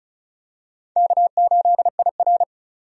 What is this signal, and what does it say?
Morse code, heard as a keyed tone, about 34 wpm, 700 Hz. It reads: K8IR
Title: USA Call Signs - 4 Characters - 35wpm